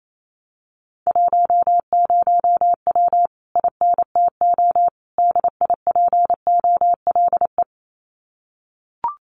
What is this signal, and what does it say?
Morse code, heard as a keyed tone, about 28 wpm, 700 Hz. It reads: 10W INTO DIPOLE
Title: Common QSO Phrases Encore 28wpm